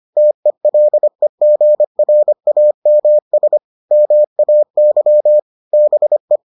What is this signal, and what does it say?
Morse code, heard as a keyed tone, about 25 wpm, 600 Hz. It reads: TELEGRAMS MAY BE